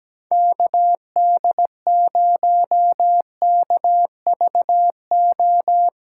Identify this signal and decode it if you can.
Morse code, heard as a keyed tone, about 17 wpm, 700 Hz. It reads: KD0KVO
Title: USA Call Signs - 6 Characters - 3x Character Spacing - 17wpm